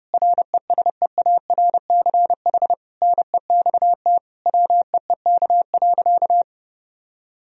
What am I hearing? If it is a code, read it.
Morse code, heard as a keyed tone, about 30 wpm, 700 Hz. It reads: RESEARCH NEXT WEEK.
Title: Sentences from Top 1000 Words - Review Part 4 - 30wpm